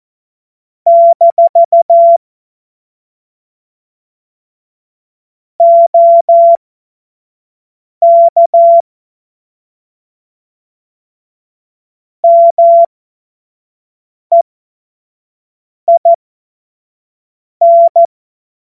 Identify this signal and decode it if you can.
Morse code, heard as a keyed tone, about 14 wpm, spaced as for 5 wpm, 680 Hz. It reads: - OK MEIN
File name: Morsekurs / Witze / Witz 06. Dezember 2022 25 F70